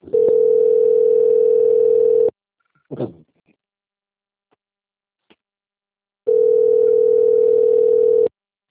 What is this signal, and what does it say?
Ringback tone (North America), 440 + 480 Hz.